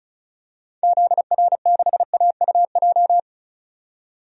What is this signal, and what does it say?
Morse code, heard as a keyed tone, about 36 wpm, 700 Hz. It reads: ZR6AUJ
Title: DXCC-100 Call Signs - 6 Characters - 35wpm